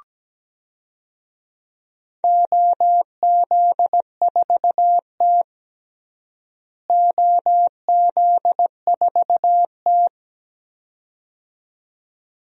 Morse code, heard as a keyed tone, about 17 wpm, 700 Hz. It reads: OZ4T OZ4T
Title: DXCC-100 Call Signs - 4 Characters - Repeated 2x - 17wpm